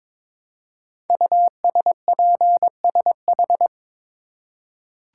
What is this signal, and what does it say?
Morse code, heard as a keyed tone, about 22 wpm, 700 Hz. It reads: USPSH
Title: Code Groups - 5 Letters - 22wpm